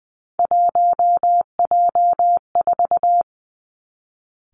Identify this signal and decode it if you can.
Morse code, heard as a keyed tone, about 20 wpm, 700 Hz. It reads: 1J4
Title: Three Challenging Letter-Number Combinations 20wpm